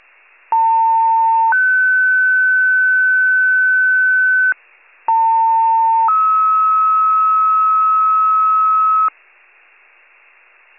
Two-tone page, heard as a one-second tone, then a three-second tone, about 900 then 1500 Hz, about 900 then 1290 Hz. It is heard from 0.5 to 9.1 s.
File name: wcalert